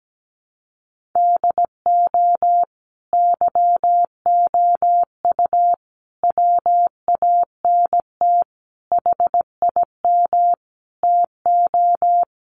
Morse code, heard as a keyed tone, about 17 wpm, 700 Hz. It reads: DO YOU WANT HIM TO
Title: Sentences from Top 1400 Words 17wpm